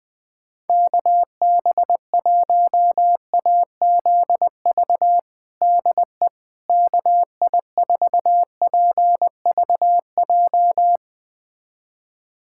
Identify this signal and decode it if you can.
Morse code, heard as a keyed tone, about 21 wpm, 700 Hz. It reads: KB1AZV DE KI4PVJ
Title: USA Callsign DE Callsign - 6 Characters - 20wpm